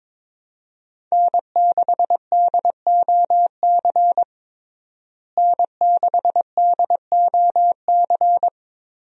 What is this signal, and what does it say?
Morse code, heard as a keyed tone, about 21 wpm, 700 Hz. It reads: N6DOC N6DOC
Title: USA Call Signs - 5 Characters - Repeated 2x - 22wpm